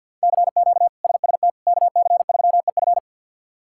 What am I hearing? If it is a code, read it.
Morse code, heard as a keyed tone, about 49 wpm, 700 Hz. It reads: CQ SST KC3EF
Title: K1USN SST QSO Practice - 50wpm